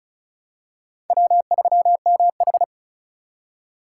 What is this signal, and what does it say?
Morse code, heard as a keyed tone, about 36 wpm, 700 Hz. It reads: W3MH
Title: USA Call Signs - 4 Characters - 3x Character Spacing - 35wpm